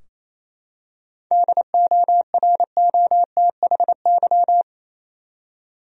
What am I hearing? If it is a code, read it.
Morse code, heard as a keyed tone, about 28 wpm, 700 Hz. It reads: DOROTHY
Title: USA Common Names - 28wpm